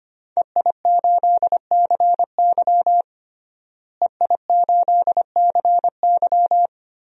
Morse code, heard as a keyed tone, about 25 wpm, 700 Hz. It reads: EI8CY EI8CY
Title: DXCC-100 Call Signs - 5 Characters - Repeated 3x - 25wpm